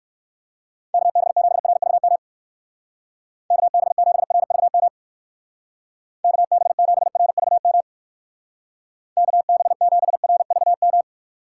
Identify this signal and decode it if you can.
Morse code, heard as a keyed tone, about 59 wpm, 700 Hz. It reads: KB7RVM KB7RVM KB7RVM KB7RVM